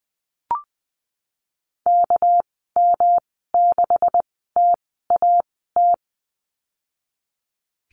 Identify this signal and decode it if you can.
Morse code, heard as a keyed tone, about 20 wpm, spaced as for 14 wpm, 700 Hz. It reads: KM6TAT
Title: 238 Lesson 38g - Letter q - Call Signs 20wpm